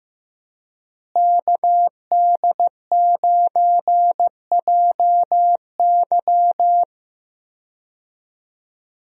Morse code, heard as a keyed tone, about 15 wpm, 700 Hz. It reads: KD9JY
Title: USA Call Signs - 5 Characters - Repeated 2x - 15wpm